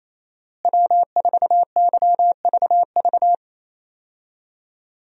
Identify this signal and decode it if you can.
Morse code, heard as a keyed tone, about 28 wpm, 700 Hz. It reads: W4YVV
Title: USA Call Signs - 5 Characters - Repeated 2x - 28wpm